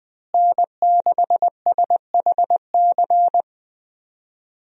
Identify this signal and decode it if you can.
Morse code, heard as a keyed tone, about 21 wpm, 700 Hz. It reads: N6SHC
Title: USA Call Signs - 5 Characters - Repeated 3x - 20wpm